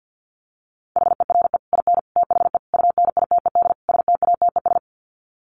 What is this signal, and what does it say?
Morse code, heard as a keyed tone, about 50 wpm, 700 Hz. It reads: HERE IN THE UNITED STATES
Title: Sets of 5 Words 50wpm